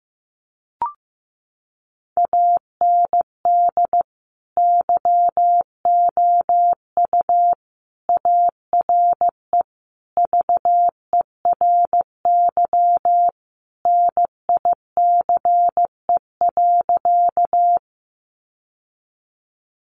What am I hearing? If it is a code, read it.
Morse code, heard as a keyed tone, about 15 wpm, 700 Hz. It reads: AND YOU ARE VERY NICE.
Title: Sentences from Top 1100 Words 15wpm